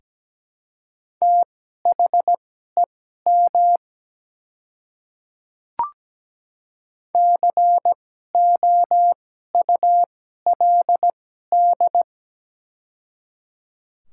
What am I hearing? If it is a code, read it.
Morse code, heard as a keyed tone, about 17 wpm, spaced as for 12 wpm, 700 Hz. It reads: THEM COULD
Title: 2x Character Spacing - Top 300 Words - 17wpm